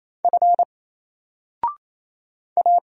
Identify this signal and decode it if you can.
Morse code, heard as a keyed tone, about 27 wpm, 700 Hz. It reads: F A
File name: Single-Letter-Number-28wpm